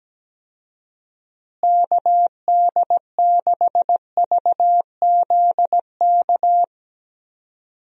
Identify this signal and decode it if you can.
Morse code, heard as a keyed tone, about 17 wpm, 700 Hz. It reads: KD6VZK
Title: USA Call Signs - 6 Characters - Repeated 3x - 17wpm